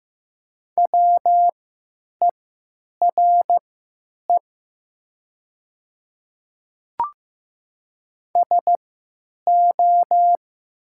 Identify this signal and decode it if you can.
Morse code, heard as a keyed tone, about 15 wpm, spaced as for 9 wpm, 700 Hz. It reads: WERE SO
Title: 3x Character Spacing - Top 100 Words - 15wpm